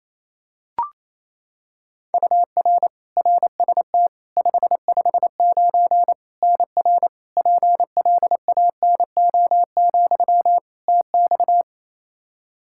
Morse code, heard as a keyed tone, about 28 wpm, 700 Hz. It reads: UR RST 559 NR PLANO, TX
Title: Practice QSO - Exchange 1 - Beginning - 28wpm